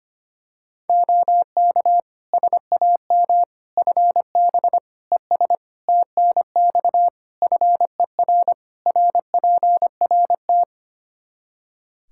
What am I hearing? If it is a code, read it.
Morse code, heard as a keyed tone, about 25 wpm, 700 Hz. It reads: OK SAM FB ES TNX FER RPRT
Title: Practice QSO - Exchange 2 - Info - 25wpm